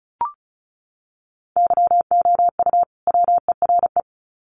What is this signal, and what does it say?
Morse code, heard as a keyed tone, about 34 wpm, 700 Hz. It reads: YOU WERE
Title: Sets of 2 Words 35wpm